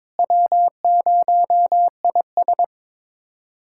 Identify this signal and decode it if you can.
Morse code, heard as a keyed tone, about 22 wpm, 700 Hz. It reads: W0IS
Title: USA Callsign DE Callsign - 4 Characters - 22wpm